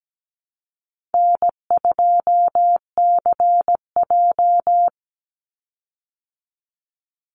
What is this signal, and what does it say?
Morse code, heard as a keyed tone, about 17 wpm, 700 Hz. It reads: N2CJ
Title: Call Signs Repeated 3x - 17wpm